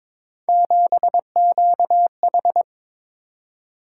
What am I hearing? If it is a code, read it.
Morse code, heard as a keyed tone, about 22 wpm, 700 Hz. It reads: 7QH
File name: practice / 22wpm / Three Letter-Number Combinations - 22wpm